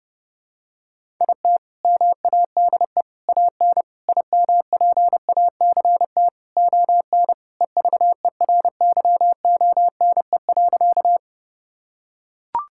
Morse code, heard as a keyed tone, about 30 wpm, 700 Hz. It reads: IT MADE AN IMPACT ON EVERYONE.
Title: Sentences from Top 1000 Words - Review Part 6 - 30wpm